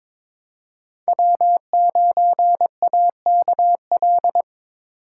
Morse code, heard as a keyed tone, about 21 wpm, 700 Hz. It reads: W9AKL
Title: USA Call Signs - 5 Characters - 4x Character Spacing - 22wpm